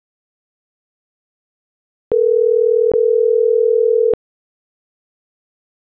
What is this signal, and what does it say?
Ringback tone (North America), 440 + 480 Hz.